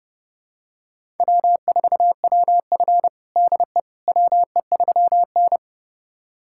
Morse code, heard as a keyed tone, about 30 wpm, 700 Hz. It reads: W4WF DE WE3N
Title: USA Callsign DE Callsign - 4 Characters - 30wpm